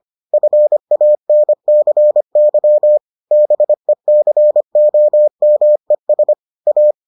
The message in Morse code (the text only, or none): FANCY BECOMES A